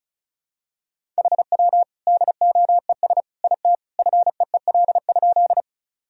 Morse code, heard as a keyed tone, about 36 wpm, 700 Hz. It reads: HW DOES IT FEEL?
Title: Sentences from Top 400 Words and Abbreviations 35wpm